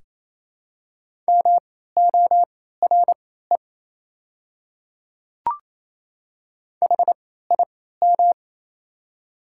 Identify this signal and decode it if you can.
Morse code, heard as a keyed tone, about 27 wpm, spaced as for 16 wpm, 690 Hz. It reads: MORE HIM
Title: 3x Character Spacing - Top 100 Words - 28wpm